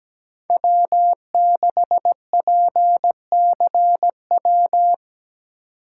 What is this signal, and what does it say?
Morse code, heard as a keyed tone, about 17 wpm, 700 Hz. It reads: W6PCW
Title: USA Call Signs - 3x Character Spacing - 17wpm